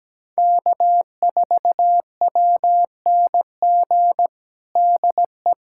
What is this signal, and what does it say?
Morse code, heard as a keyed tone, about 17 wpm, 700 Hz. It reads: K4WNG DE
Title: USA Callsign DE Callsign 17wpm